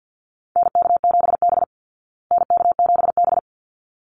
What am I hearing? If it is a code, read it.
Morse code, heard as a keyed tone, about 48 wpm, 700 Hz. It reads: NK7B NK7B
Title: Call Signs Repeated 3x - 50wpm